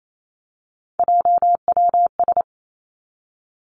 Morse code, heard as a keyed tone, about 28 wpm, 700 Hz. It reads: JWS
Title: Three Challenging Letter-Number Combinations 28wpm